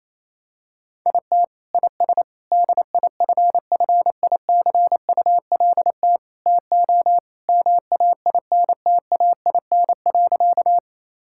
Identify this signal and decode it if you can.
Morse code, heard as a keyed tone, about 28 wpm, 700 Hz. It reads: IT IS DIFFICULT TO MAINTAIN.
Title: Sentences from Top 1000 Words - Review Part 5 - 28wpm